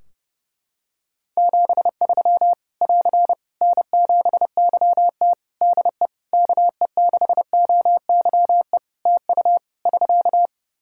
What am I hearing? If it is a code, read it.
Morse code, heard as a keyed tone, about 30 wpm, 700 Hz. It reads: 73 <AR> N7YT DE KE6OYE TU <SK>
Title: Practice QSO - Exchange 4 - Ending - 30wpm